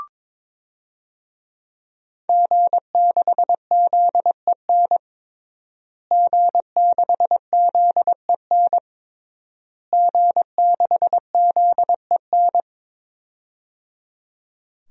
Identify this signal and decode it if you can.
Morse code, heard as a keyed tone, about 22 wpm, 700 Hz. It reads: G6ZEN G6ZEN G6ZEN
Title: DXCC-100 Call Signs - 5 Characters - Repeated 3x - 22wpm